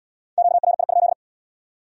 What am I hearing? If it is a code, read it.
Morse code, heard as a keyed tone, about 76 wpm, 700 Hz. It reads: QRI?